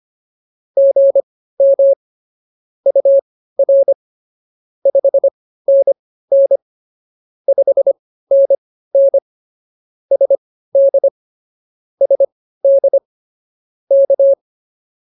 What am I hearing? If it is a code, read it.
Morse code, heard as a keyed tone, about 24 wpm, spaced as for 15 wpm, 550 Hz. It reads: GM UR 5NN 5NN SD SD K
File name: POTA101 15